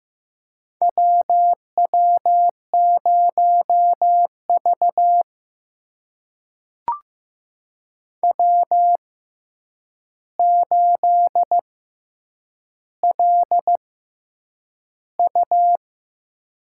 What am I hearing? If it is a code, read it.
Morse code, heard as a keyed tone, about 15 wpm, spaced as for 8 wpm, 700 Hz. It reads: WW0V W8LU
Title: USA Call Signs - 4 Characters - 6x Character Spacing - 15wpm